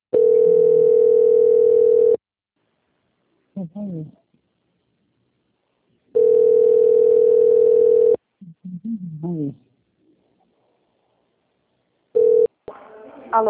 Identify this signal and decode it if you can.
Ringback tone (North America), 440 + 480 Hz, with speech.
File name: out- - - - - .43191